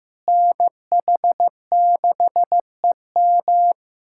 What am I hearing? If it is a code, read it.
Morse code, heard as a keyed tone, about 16 wpm, 700 Hz. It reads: NH6EM
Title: USA Callsign DE Callsign 15wpm